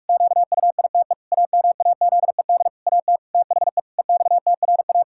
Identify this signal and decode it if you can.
Morse code, heard as a keyed tone, about 46 wpm, 700 Hz. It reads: QUITE AMAZED AT THE EXTRA
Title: Pangram Challenge v2 - 45wpm